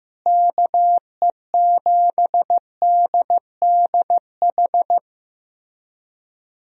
Morse code, heard as a keyed tone, about 15 wpm, 700 Hz. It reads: KE7DDH
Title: USA Call Signs - 6 Characters - Repeated 3x - 15wpm